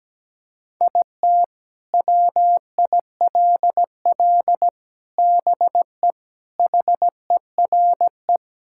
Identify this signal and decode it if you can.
Morse code, heard as a keyed tone, about 17 wpm, 700 Hz. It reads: IT WILL BE HERE
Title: Sentences from Top 1000 Words - Review Part 5 - 17wpm